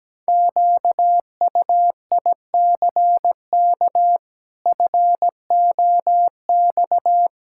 Morse code, heard as a keyed tone, about 17 wpm, 700 Hz. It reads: QUICK FOX
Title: Pangram Challenge v2 - 17wpm